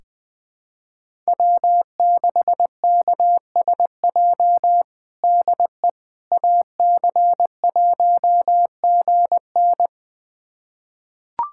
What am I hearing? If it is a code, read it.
Morse code, heard as a keyed tone, about 20 wpm, 700 Hz. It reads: W6KSJ DE AC1GN
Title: USA Callsign DE Callsign - 5 Characters - 20wpm